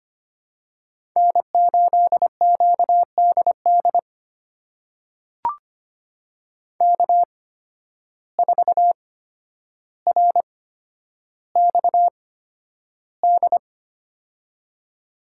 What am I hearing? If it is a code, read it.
Morse code, heard as a keyed tone, about 25 wpm, spaced as for 11 wpm, 700 Hz. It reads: N8QDD K4RXD
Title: USA Call Signs - 5 Characters - 8x Character Spacing - 25wpm